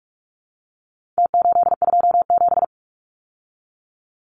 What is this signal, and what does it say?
Morse code, heard as a keyed tone, about 44 wpm, 700 Hz. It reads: T827
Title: Numbers - Contest Style - 4-Digit 45wpm